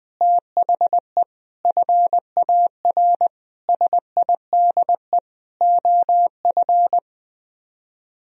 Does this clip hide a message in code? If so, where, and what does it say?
Yes: Morse code from 0.2 to 7.0 s, heard as a keyed tone, about 21 wpm, 700 Hz. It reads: THE FAR SIDE OF